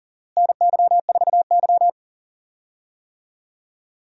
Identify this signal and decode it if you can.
Morse code, heard as a keyed tone, about 37 wpm, 700 Hz. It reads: NY4Y
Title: USA Call Signs - 4 Characters - Repeated 3x - 40wpm